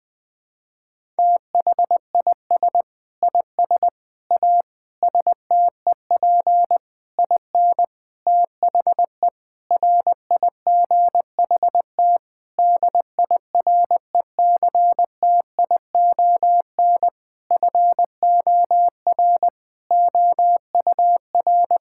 Morse code, heard as a keyed tone, about 20 wpm, 700 Hz. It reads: THIS IS A STEP IN THE RIGHT DIRECTION FOR OUR
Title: Sentences with 8-Word Phrases - Spoken Beginning - 20wpm